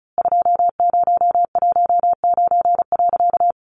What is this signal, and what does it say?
Morse code, heard as a keyed tone, about 34 wpm, 700 Hz. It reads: 2019.
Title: A Sound Of Thunder 35wpm